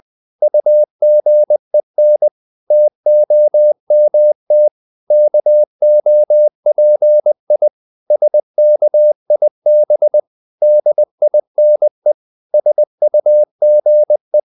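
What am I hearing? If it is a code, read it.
Morse code, heard as a keyed tone, about 20 wpm, 600 Hz. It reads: UGEN TOMT KOPI SKIB DINE SUGE
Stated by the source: Morse code practice of short Danish words at 20 wpm, from CW med Gnister.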